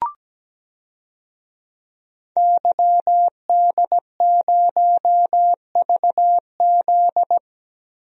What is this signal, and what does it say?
Morse code, heard as a keyed tone, about 17 wpm, 700 Hz. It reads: YD0VZ
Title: DXCC-100 Call Signs - 5 Characters - Repeated 2x - 17wpm